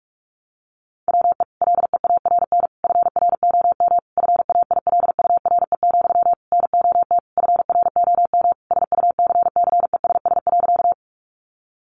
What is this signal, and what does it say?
Morse code, heard as a keyed tone, about 44 wpm, 700 Hz. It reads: WE LEARN FROM FAILURE, NOT FROM SUCCESS.
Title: Sentences from Top 900 Words 45wpm